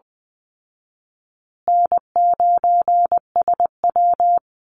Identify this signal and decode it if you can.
Morse code, heard as a keyed tone, about 20 wpm, 700 Hz. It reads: N9SW
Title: Easy Call Signs Repeated 2x - 20wpm